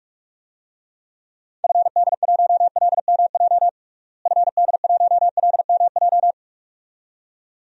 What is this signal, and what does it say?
Morse code, heard as a keyed tone, about 41 wpm, 700 Hz. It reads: FD1LMJ FD1LMJ
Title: DXCC-100 Call Signs - 6 Characters - Repeated 2x - 45wpm